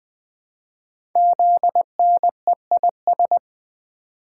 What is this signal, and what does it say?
Morse code, heard as a keyed tone, about 20 wpm, 700 Hz. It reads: ZNEIS
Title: Code Groups - 5 Letters - 20wpm